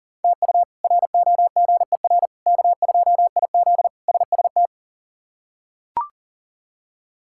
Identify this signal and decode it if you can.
Morse code, heard as a keyed tone, about 40 wpm, 700 Hz. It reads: TU ROGER K2IZ SST
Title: K1USN SST QSO Practice - 40wpm